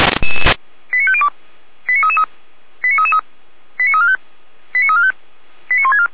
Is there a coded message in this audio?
Yes: five-tone selective call (ZVEI) from 0.9 to 6.1 s, heard as quick runs of five tones, tones of about 70 ms. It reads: 89502 89333 89333 89345 89345 89164